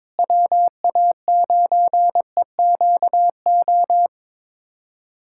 Morse code, heard as a keyed tone, about 22 wpm, 700 Hz. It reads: WA9EQO
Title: USA Callsign DE Callsign 22wpm